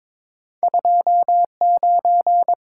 Morse code, heard as a keyed tone, about 30 wpm, spaced as for 26 wpm, 700 Hz. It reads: 29
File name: Speed-Racing - Two Letter-Number Combinations - 22wpm